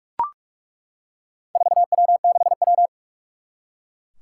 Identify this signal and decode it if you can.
Morse code, heard as a keyed tone, about 41 wpm, 700 Hz. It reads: 4W6W
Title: Four Challenging Letter-Number Combinations 45wpm